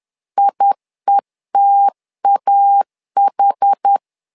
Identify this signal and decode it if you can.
Morse code, heard as a keyed tone, about 10 wpm, 800 Hz. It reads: IETAH
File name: LEZIONI / 2 UHDN / 2 UHD 10wpm 1826